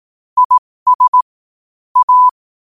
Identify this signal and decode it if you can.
Morse code, heard as a keyed tone, about 18 wpm, spaced as for 15 wpm, 1000 Hz. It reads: IS A
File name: qotdfiles / 15wpm